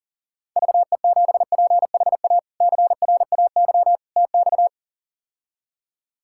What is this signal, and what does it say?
Morse code, heard as a keyed tone, about 37 wpm, 700 Hz. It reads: VE7PHA CRAY TX
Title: K1USN SST QSO Practice - 40wpm